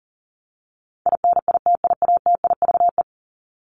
Morse code, heard as a keyed tone, about 40 wpm, 700 Hz. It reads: INITIATIVE
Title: Top 1751-2000 Words 40wpm